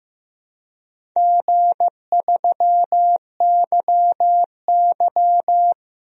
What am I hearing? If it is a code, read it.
Morse code, heard as a keyed tone, about 15 wpm, 700 Hz. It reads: G3YY
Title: DXCC-100 Call Signs - 4 Characters - 15wpm